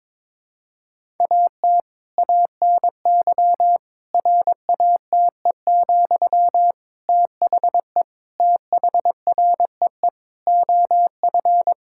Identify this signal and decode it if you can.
Morse code, heard as a keyed tone, about 22 wpm, 700 Hz. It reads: AT ANY RATE, THE THREE OF
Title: Sentences from Top 1000 Words - Review Part 6 - 22wpm